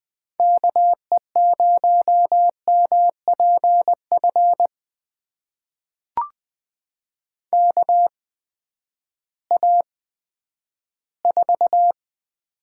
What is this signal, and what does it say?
Morse code, heard as a keyed tone, about 20 wpm, spaced as for 11 wpm, 700 Hz. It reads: KE0MPF KA4
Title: USA Call Signs - 6 Characters - 8x Character Spacing - 20wpm